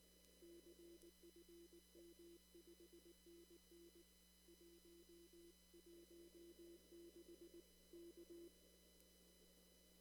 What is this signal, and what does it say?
Morse code, heard as a keyed tone, about 19 wpm, 340 Hz. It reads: CFM 5NN 116 K
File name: CW / kt5j wpx cw-op-k5pi